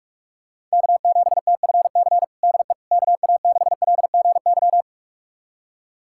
Morse code, heard as a keyed tone, about 45 wpm, 700 Hz. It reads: K7TFC DE KA6LGY
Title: USA Callsign DE Callsign 45wpm